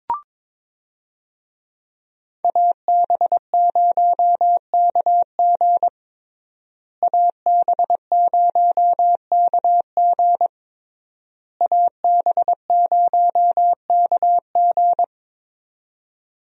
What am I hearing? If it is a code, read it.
Morse code, heard as a keyed tone, about 22 wpm, 690 Hz. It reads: AB0KG AB0KG AB0KG
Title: USA Call Signs - 5 Characters - Repeated 3x - 22wpm